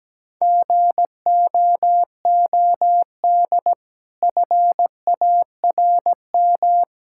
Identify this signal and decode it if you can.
Morse code, heard as a keyed tone, about 17 wpm, 700 Hz. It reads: GOOD FARM
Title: Sentences from Top 1000 Words - Review Part 5 - 17wpm